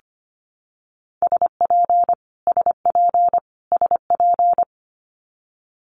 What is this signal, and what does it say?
Morse code, heard as a keyed tone, about 25 wpm, 700 Hz. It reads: SP SP SP
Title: Multi-letter Phonemes Repeated 3x 25wpm